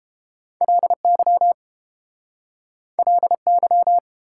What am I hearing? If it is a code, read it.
Morse code, heard as a keyed tone, about 31 wpm, 700 Hz. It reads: LY LY